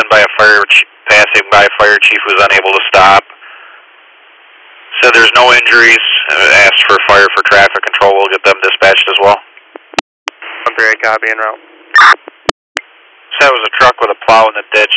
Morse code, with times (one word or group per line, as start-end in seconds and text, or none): none